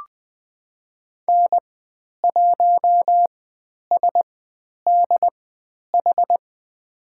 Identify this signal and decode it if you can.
Morse code, heard as a keyed tone, about 20 wpm, spaced as for 10 wpm, 700 Hz. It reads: N1SDH
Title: 102 Lesson 17g - Letter u - Call Signs 20wpm